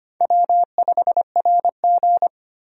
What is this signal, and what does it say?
Morse code, heard as a keyed tone, about 25 wpm, 700 Hz. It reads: W5RG
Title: USA Call Signs - 4 Characters - 25wpm